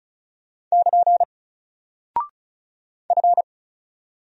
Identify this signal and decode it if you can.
Morse code, heard as a keyed tone, about 36 wpm, 700 Hz. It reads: <KN> F
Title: Single Character (Letter:Number:Punctuation:Prosign) Alt 35wpm